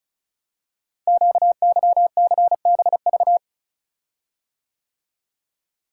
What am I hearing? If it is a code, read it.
Morse code, heard as a keyed tone, about 36 wpm, 690 Hz. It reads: QYCBV
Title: Code Groups - 5 Letters - 35wpm